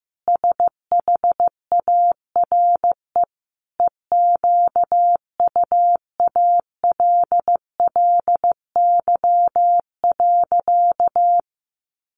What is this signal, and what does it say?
Morse code, heard as a keyed tone, about 15 wpm, 700 Hz. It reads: SHARE EQUALLY.
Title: Sentences from Top 1400 Words 15wpm